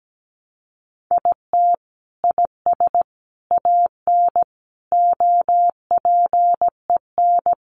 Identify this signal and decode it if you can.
Morse code, heard as a keyed tone, about 17 wpm, 700 Hz. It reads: IT IS AN OPEN
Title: Sentences from Top 1300 Words 17wpm